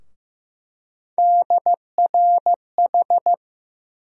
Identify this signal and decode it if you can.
Morse code, heard as a keyed tone, about 15 wpm, 700 Hz. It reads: DRH
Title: USA Call Sign Postfixes - 3 Letters 15wpm